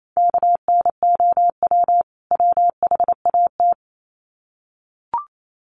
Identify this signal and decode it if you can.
Morse code, heard as a keyed tone, about 28 wpm, 700 Hz. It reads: KNOW WHAT
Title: Sets of 2 Words 28wpm